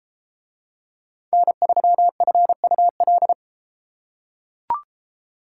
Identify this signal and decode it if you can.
Morse code, heard as a keyed tone, about 31 wpm, 700 Hz. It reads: N3FUL